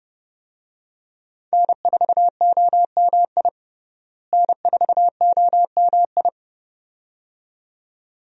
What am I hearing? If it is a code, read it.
Morse code, heard as a keyed tone, about 30 wpm, 700 Hz. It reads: N4OMI N4OMI
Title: USA Call Signs - 5 Characters - Repeated 2x - 30wpm